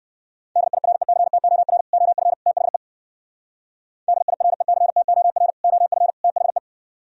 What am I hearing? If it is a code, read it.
Morse code, heard as a keyed tone, about 58 wpm, 700 Hz. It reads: DIRECTOR OF THE DIRECTOR OF THE